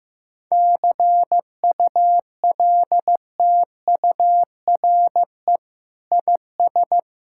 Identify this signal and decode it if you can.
Morse code, heard as a keyed tone, about 15 wpm, 700 Hz. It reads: CULTURE IS
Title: Sentences from Top 1000 Words - Review Part 6 - 15wpm